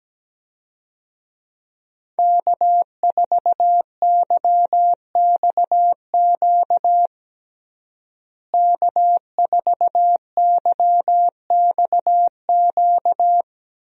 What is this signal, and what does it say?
Morse code, heard as a keyed tone, about 17 wpm, 700 Hz. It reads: K4YXQ K4YXQ
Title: USA Call Signs - 5 Characters - Repeated 3x - 17wpm